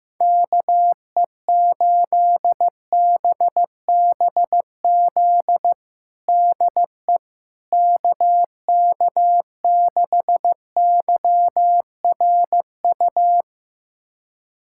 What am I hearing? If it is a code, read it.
Morse code, heard as a keyed tone, about 15 wpm, 700 Hz. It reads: KE8BBZ DE KK6YRU
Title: USA Callsign DE Callsign - 6 Characters - 15wpm